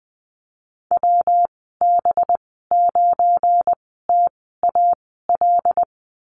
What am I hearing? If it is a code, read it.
Morse code, heard as a keyed tone, about 20 wpm, spaced as for 14 wpm, 700 Hz. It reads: WB9TAL
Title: 193 Lesson 31g - Letter v - Call Signs 20wpm